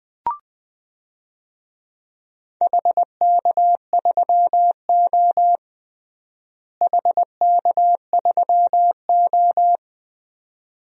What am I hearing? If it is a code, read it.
Morse code, heard as a keyed tone, about 20 wpm, 700 Hz. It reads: HK3O HK3O
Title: DXCC-100 Call Signs - 4 Characters - Repeated 2x - 20wpm